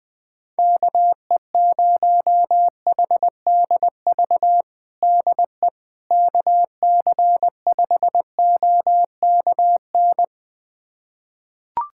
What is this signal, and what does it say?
Morse code, heard as a keyed tone, about 20 wpm, 700 Hz. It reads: KE0HDV DE KC5OKN
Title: USA Callsign DE Callsign 20wpm